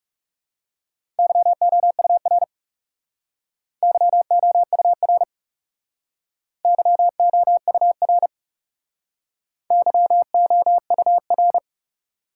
Morse code, heard as a keyed tone, about 36 wpm, 700 Hz. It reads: YOUR YOUR YOUR YOUR
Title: Speed-Racing Top 180 QSO Elements 30wpm